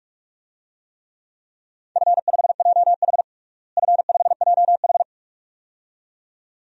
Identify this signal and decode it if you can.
Morse code, heard as a keyed tone, about 43 wpm, 700 Hz. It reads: F5JH F5JH
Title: DXCC-100 Call Signs - 4 Characters - Repeated 2x - 45wpm